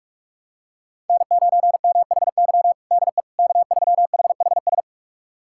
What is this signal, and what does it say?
Morse code, heard as a keyed tone, about 41 wpm, 700 Hz. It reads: N9MHY DE K3HHS
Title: USA Callsign DE Callsign - 5 Characters - 45wpm